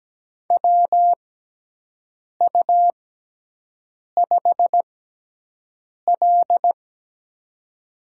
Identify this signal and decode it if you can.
Morse code, heard as a keyed tone, about 17 wpm, spaced as for 6 wpm, 700 Hz. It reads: WU5L
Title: USA Call Signs - 4 Characters - 6x Character Spacing - 17wpm